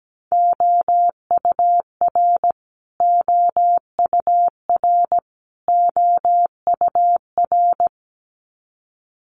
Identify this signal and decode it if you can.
Morse code, heard as a keyed tone, about 17 wpm, 700 Hz. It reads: OUR OUR OUR
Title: Top 500 Words Repeated 17wpm